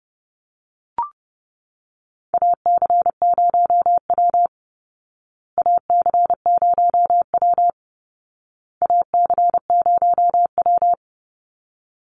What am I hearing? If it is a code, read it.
Morse code, heard as a keyed tone, about 28 wpm, 700 Hz. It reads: AC0W AC0W AC0W
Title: Call Signs Repeated 2x - 30wpm